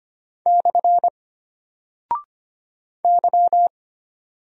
Morse code, heard as a keyed tone, about 25 wpm, 700 Hz. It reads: / Y
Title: Single Character (Letter:Number:Punctuation:Prosign) Alt 25wpm